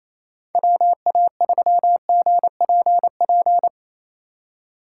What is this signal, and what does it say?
Morse code, heard as a keyed tone, about 28 wpm, 700 Hz. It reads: WA3GPP
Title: K1USN SST QSO Practice - 28wpm